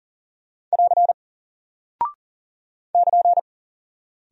Morse code, heard as a keyed tone, about 40 wpm, 700 Hz. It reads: <AR> <KN>
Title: Single Character (Letter:Number:Punctuation:Prosign) Alt 40wpm